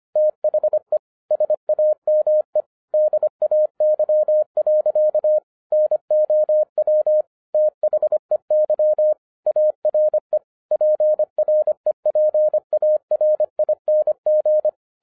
Morse code, heard as a keyed tone, about 24 wpm, 610 Hz. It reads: THE SAME DAY. NOW THEY ARE PREPARING